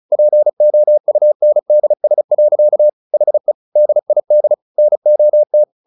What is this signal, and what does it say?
Morse code, heard as a keyed tone, about 34 wpm, 600 Hz. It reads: POUNDS. HE DID NOT
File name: the old man and the sea / the old man and the sea 1 35wpm